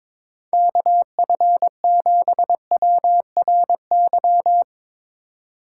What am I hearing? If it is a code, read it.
Morse code, heard as a keyed tone, about 22 wpm, 700 Hz. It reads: KF7WRY
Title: USA Call Signs - 6 Characters - 3x Character Spacing - 22wpm